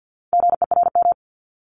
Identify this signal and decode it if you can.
Morse code, heard as a keyed tone, about 50 wpm, 700 Hz. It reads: GERM
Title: Multi-letter Phonemes Repeated 3x 50wpm